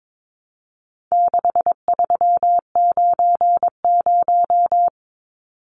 Morse code, heard as a keyed tone, about 22 wpm, 700 Hz. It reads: 6390
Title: Numbers - 4-Digit 22wpm